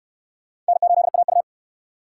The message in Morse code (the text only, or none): N8AU